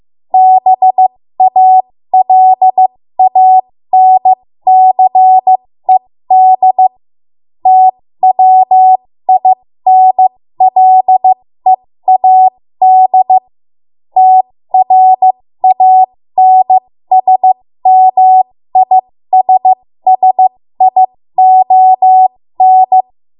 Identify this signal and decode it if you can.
Morse code, heard as a keyed tone, about 15 wpm, spaced as for 13 wpm, 750 Hz. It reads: BALANCED TWINLEAD TRANSMISSION
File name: Morse / 13WPM